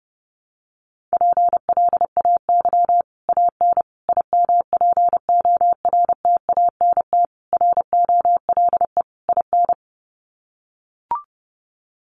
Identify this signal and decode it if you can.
Morse code, heard as a keyed tone, about 30 wpm, 700 Hz. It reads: PLAY AN IMPORTANT ROLE IN
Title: Sets of 5 Words Encore 30wpm